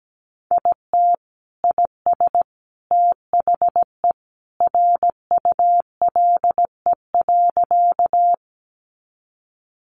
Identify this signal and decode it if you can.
Morse code, heard as a keyed tone, about 17 wpm, 700 Hz. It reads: IT IS THE RULE.
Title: Sentences from Top 400 Words 17wpm